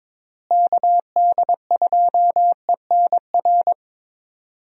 Morse code, heard as a keyed tone, about 22 wpm, 700 Hz. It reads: KD2ENR
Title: USA Call Signs - 6 Characters - Repeated 2x - 22wpm